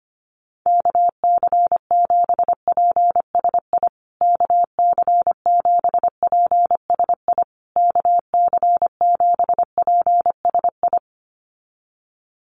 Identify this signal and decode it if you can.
Morse code, heard as a keyed tone, about 25 wpm, 700 Hz. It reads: KC7PSI KC7PSI KC7PSI
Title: USA Call Signs Repeated 3x 25wpm